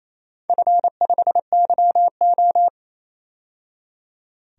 Morse code, heard as a keyed tone, about 28 wpm, 700 Hz. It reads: F5YO
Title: DXCC-100 Call Signs - 4 Characters - 28wpm